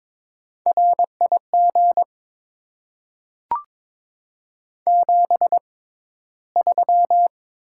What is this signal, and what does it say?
Morse code, heard as a keyed tone, about 21 wpm, spaced as for 13 wpm, 700 Hz. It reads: RIG 73
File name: Top 20 QSO Elements - 6x Character Spacing - 22wpm